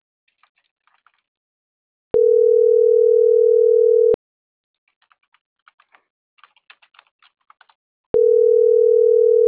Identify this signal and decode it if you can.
Ringback tone (North America), 440 + 480 Hz.